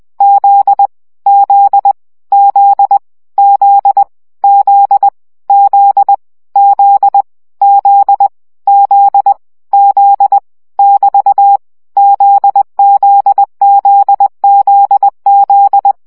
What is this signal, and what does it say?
Morse code, heard as a keyed tone, about 20 wpm, 800 Hz. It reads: Z Z Z Z Z Z Z Z Z Z <BT> ZZZZZ
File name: CW gyakorlas z 17 baud